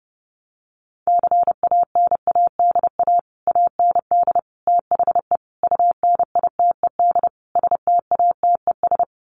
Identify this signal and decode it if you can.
Morse code, heard as a keyed tone, about 30 wpm, 700 Hz. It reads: CANADA AND THE UNITED STATES
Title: Sets of 5 Words Encore 30wpm